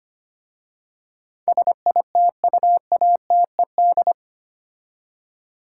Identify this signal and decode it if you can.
Morse code, heard as a keyed tone, about 25 wpm, 700 Hz. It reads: SITUATED
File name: 8 Letter Words - First 3 Letters Given - 25wpm